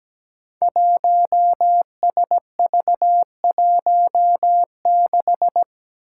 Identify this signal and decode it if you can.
Morse code, heard as a keyed tone, about 17 wpm, 700 Hz. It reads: 1SV16
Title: Five Challenging Letter-Number Combinations 17wpm